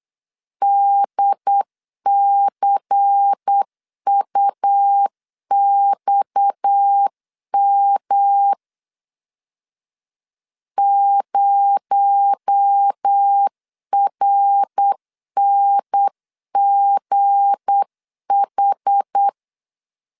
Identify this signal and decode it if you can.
Morse code, heard as a keyed tone, about 8 wpm, 800 Hz. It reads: DCUXM 0RNGH
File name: LEZIONI / 7 GF9Z / 7 G 8wpm 3118